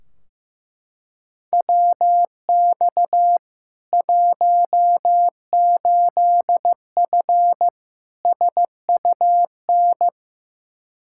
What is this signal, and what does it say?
Morse code, heard as a keyed tone, about 16 wpm, 700 Hz. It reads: WX 18F SUN
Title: Practice QSO - Exchange 2 - Info - 15wpm